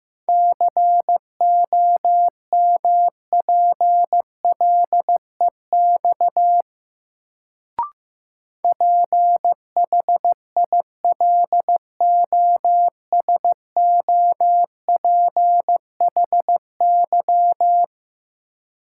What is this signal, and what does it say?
Morse code, heard as a keyed tone, about 15 wpm, 700 Hz. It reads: COMPLEX PHILOSOPHY
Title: Sentences from Top 900 Words - New Word Drill - 15wpm